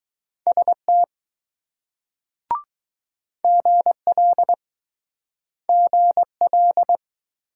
Morse code, heard as a keyed tone, about 20 wpm, 700 Hz. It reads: ST GL GL